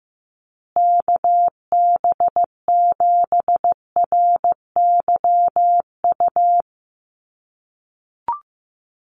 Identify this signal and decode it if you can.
Morse code, heard as a keyed tone, about 14 wpm, 700 Hz. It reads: KB7RYU
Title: USA Call Signs Repeated 2x 15wpm